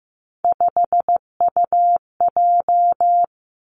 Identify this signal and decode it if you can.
Morse code, heard as a keyed tone, about 15 wpm, 700 Hz. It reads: 5UJ
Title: Three Challenging Letter-Number Combinations 15wpm